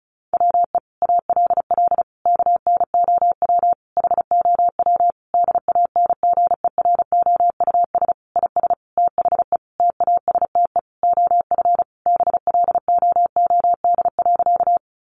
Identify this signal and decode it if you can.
Morse code, heard as a keyed tone, about 34 wpm, 700 Hz. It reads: WE ALL KNOW HOW DANGEROUS IS THE TASTE OF BLOOD.
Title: Sentences from Top 1200 Words 35wpm